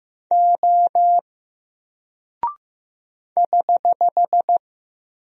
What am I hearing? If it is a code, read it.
Morse code, heard as a keyed tone, about 15 wpm, 700 Hz. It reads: O <HH>
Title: Single Character (Letter:Number:Punctuation:Prosign) 15wpm